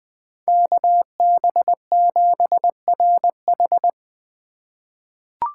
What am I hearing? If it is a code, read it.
Morse code, heard as a keyed tone, about 20 wpm, 700 Hz. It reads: KB7RH
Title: USA Call Signs - 5 Characters - 8x Character Spacing - 20wpm